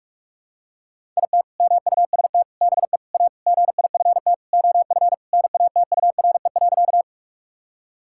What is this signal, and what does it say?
Morse code, heard as a keyed tone, about 41 wpm, 700 Hz. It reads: IT MUST BE A GIFT OF NATURE.
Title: Sentences from Top 1000 Words - Review Part 6 - 45wpm